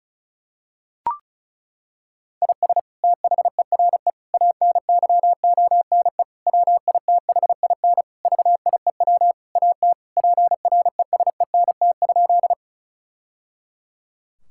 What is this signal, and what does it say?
Morse code, heard as a keyed tone, about 34 wpm, 700 Hz. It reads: IS THERE ANYONE WITHIN VIEW AT PRESENT?
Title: Sentences from Top 1000 Words - Review Part 6 - 35wpm